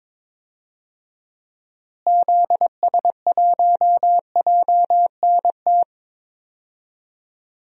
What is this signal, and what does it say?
Morse code, heard as a keyed tone, about 22 wpm, 700 Hz. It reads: ZS1JNT
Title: DXCC-100 Call Signs - 6 Characters - Repeated 3x - 22wpm